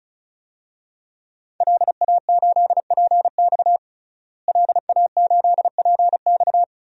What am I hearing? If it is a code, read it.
Morse code, heard as a keyed tone, about 36 wpm, 700 Hz. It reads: LA8PX LA8PX
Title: DXCC-100 Call Signs - 5 Characters - Repeated 3x - 35wpm